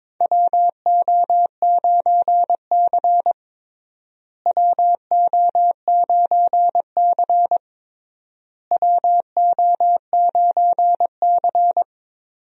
Morse code, heard as a keyed tone, about 22 wpm, 700 Hz. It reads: WO9C WO9C WO9C
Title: USA Call Signs - 4 Characters - Repeated 3x - 22wpm